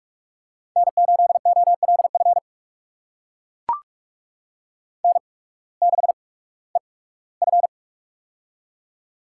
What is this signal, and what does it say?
Morse code, heard as a keyed tone, about 44 wpm, spaced as for 21 wpm, 700 Hz. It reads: N8OLF N6EF
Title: USA Call Signs - 8x Character Spacing - 45wpm